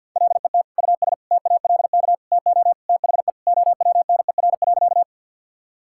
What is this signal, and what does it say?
Morse code, heard as a keyed tone, about 51 wpm, 700 Hz. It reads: LET US TALK TO THE OWNER.
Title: Sentences from Top 1600 Words 50wpm